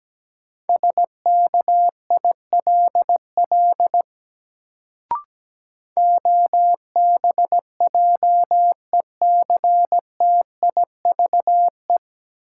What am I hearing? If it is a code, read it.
Morse code, heard as a keyed tone, about 17 wpm, 700 Hz. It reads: SKILL OBJECTIVE
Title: Sentences from Top 1200 Words - New Word Drill - 17wpm